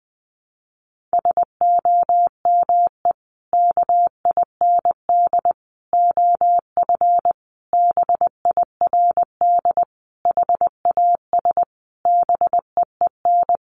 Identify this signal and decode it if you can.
Morse code, heard as a keyed tone, about 20 wpm, 700 Hz. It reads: SOME KIND OF BIRD HAS BEEN
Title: Sentences from Top 1200 Words 20wpm